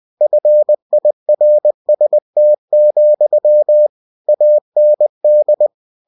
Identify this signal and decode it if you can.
Morse code, heard as a keyed tone, about 20 wpm, 600 Hz. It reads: FIRST, AND
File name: cw-books / winnie-the-pooh / winnie-20wpm-page-0051